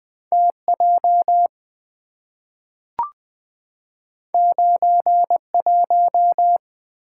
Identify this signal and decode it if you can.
Morse code, heard as a keyed tone, about 20 wpm, 690 Hz. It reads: TJ 91
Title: Two Letter-Number Combinations - 20wpm